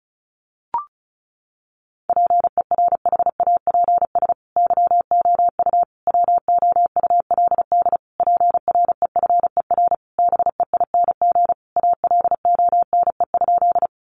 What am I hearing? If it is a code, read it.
Morse code, heard as a keyed tone, about 34 wpm, 700 Hz. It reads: PERHAPS YOU WOULD PREFER BEING ALONE?
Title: Sentences from Top 1100 Words 35wpm